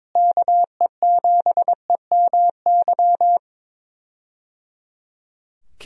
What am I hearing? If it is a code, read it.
Morse code, heard as a keyed tone, about 22 wpm, 700 Hz. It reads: KE7EMY
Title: USA Call Signs - 6 Characters - Repeated 3x - 22wpm